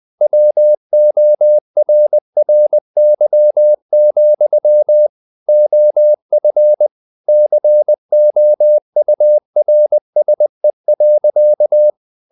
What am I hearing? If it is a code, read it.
Morse code, heard as a keyed tone, about 20 wpm, 600 Hz. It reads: WORRY, OF COURSE.
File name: the old man and the sea part 2 20wpm